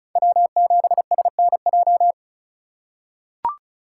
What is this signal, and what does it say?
Morse code, heard as a keyed tone, about 36 wpm, 700 Hz. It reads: W7SNJ
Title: USA Call Signs - 3x Character Spacing - 35wpm